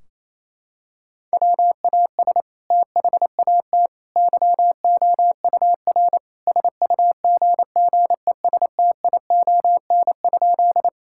Morse code, heard as a keyed tone, about 28 wpm, 700 Hz. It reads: WAS THAT YOUR SUGGESTION?
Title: Sentences from Top 1600 Words 28wpm